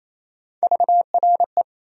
Morse code, heard as a keyed tone, about 28 wpm, 700 Hz. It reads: VRE
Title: USA Call Sign Postfixes - 3 Letters 28wpm